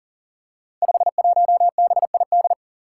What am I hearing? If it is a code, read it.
Morse code, heard as a keyed tone, about 38 wpm, 700 Hz. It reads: 51BID
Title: Five Challenging Letter-Number Combinations 40wpm